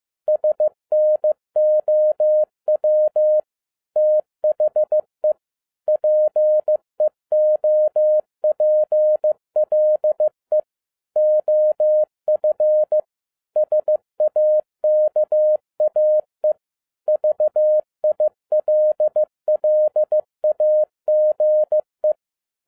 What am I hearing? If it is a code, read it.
Morse code, heard as a keyed tone, about 15 wpm, 610 Hz. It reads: SNOW THE PEOPLE OF SAKAE VILLAGE
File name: 2011-10-mp3 / n02-FIELD DAY CONTEST QRV FROM SAKAE VILLAGE 15wpm